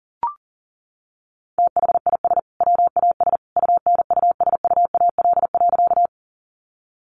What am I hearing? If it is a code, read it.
Morse code, heard as a keyed tone, about 40 wpm, 700 Hz. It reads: THIS WAS UNUSUAL.
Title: Sentences from Top 1400 Words 40wpm